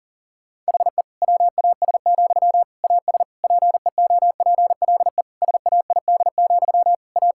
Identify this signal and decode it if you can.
Morse code, heard as a keyed tone, about 40 wpm, 700 Hz. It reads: HE WAS, AS PEOPLE SAID, A
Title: Sentences from Top 1000 Words - Review Part 6 - 40wpm